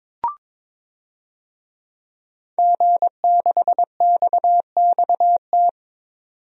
Morse code, heard as a keyed tone, about 22 wpm, 700 Hz. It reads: G6XXT
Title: DXCC-100 Call Signs - 5 Characters - 22wpm